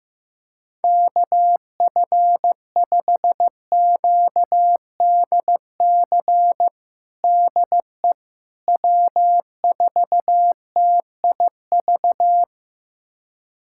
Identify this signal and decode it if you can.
Morse code, heard as a keyed tone, about 16 wpm, 700 Hz. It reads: KF5QDC DE W4TIV
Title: USA Callsign DE Callsign 15wpm